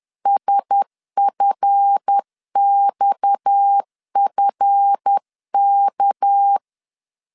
Morse code, heard as a keyed tone, about 10 wpm, 800 Hz. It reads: SFXFK
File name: LEZIONI / 7 GF9Z / 7 GF 10wpm 2514